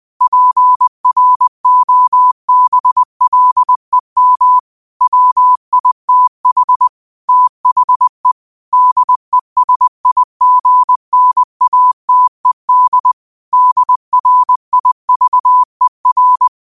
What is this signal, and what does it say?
Morse code, heard as a keyed tone, about 20 wpm, 1000 Hz. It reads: PROBLEM WITH THE DESIGNATED DRIVER
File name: qotdfiles / 20wpm